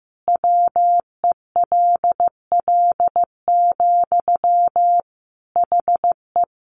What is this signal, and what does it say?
Morse code, heard as a keyed tone, about 15 wpm, 700 Hz. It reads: WELL, HE
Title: Sentences from Top 700 Words 15wpm